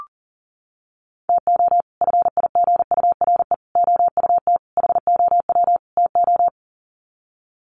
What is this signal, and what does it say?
Morse code, heard as a keyed tone, about 40 wpm, 700 Hz. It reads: TO FIGURE OUT HOW TO
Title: Sets of 5 Words 40wpm